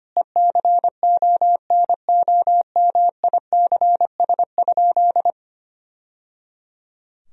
Morse code, heard as a keyed tone, about 25 wpm, 700 Hz. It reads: ECONOMICS?
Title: Sentences from Top 1000 Words - Review Part 3 - 25wpm